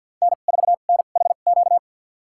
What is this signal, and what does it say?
Morse code, heard as a keyed tone, about 52 wpm, spaced as for 36 wpm, 700 Hz. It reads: N4NHQ
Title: USA Call Signs - 5 Characters - 2x Character Spacing - 50wpm